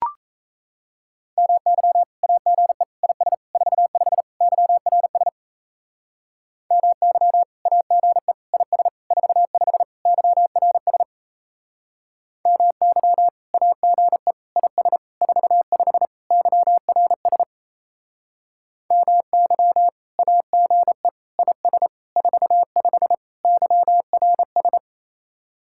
Morse code, heard as a keyed tone, about 36 wpm, 700 Hz. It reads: MY AGE IS 45 YRS MY AGE IS 45 YRS MY AGE IS 45 YRS MY AGE IS 45 YRS